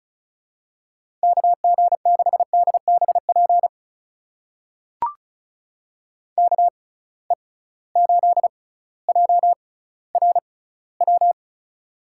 Morse code, heard as a keyed tone, about 36 wpm, spaced as for 18 wpm, 700 Hz. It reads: KG6DBP KE8JRW
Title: USA Call Signs - 6x Character Spacing - 35wpm